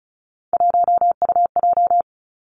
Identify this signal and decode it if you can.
Morse code, heard as a keyed tone, about 36 wpm, 700 Hz. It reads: 1UJ
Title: Three Challenging Letter-Number Combinations 35wpm